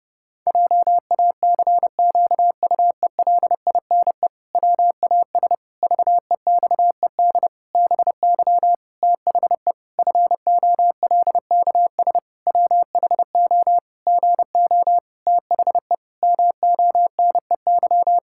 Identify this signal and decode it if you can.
Morse code, heard as a keyed tone, about 30 wpm, 700 Hz. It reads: JACQUELINE WAS VEXED BY THE FOLKS WHO GO THE MONEY
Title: Pangram Challenge v2 - 30wpm